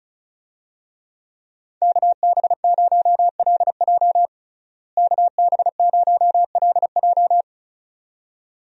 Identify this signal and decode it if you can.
Morse code, heard as a keyed tone, about 34 wpm, 700 Hz. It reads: KB0LJ KB0LJ
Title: USA Call Signs - 5 Characters - Repeated 2x - 35wpm